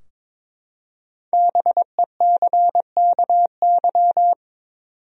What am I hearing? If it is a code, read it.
Morse code, heard as a keyed tone, about 22 wpm, 700 Hz. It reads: BECKY
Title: USA Common Names - 22wpm